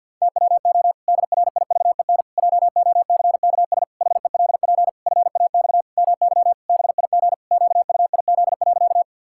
Morse code, heard as a keyed tone, about 53 wpm, 700 Hz. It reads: TWO DRIVEN JOCKS HELP FAX MY BIG QUIZ.
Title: Pangram Challenge v2 - 50wpm